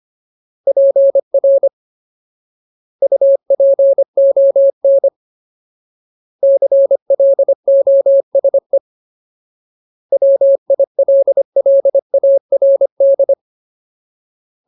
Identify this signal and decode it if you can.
Morse code, heard as a keyed tone, about 25 wpm, 550 Hz. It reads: PR UPON CLOSE WILLARD
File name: Ultimate Word List / 25WPM Ultimate